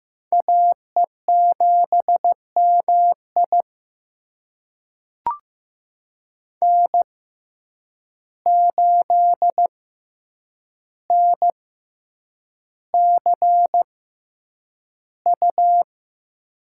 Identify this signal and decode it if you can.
Morse code, heard as a keyed tone, about 15 wpm, spaced as for 8 wpm, 700 Hz. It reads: AE7MI N8NCU
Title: USA Call Signs - 5 Characters - 6x Character Spacing - 15wpm